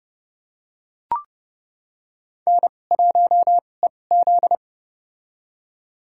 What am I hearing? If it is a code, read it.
Morse code, heard as a keyed tone, about 32 wpm, spaced as for 22 wpm, 700 Hz. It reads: N1EZ
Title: USA Call Signs - 2x Character Spacing - 30wpm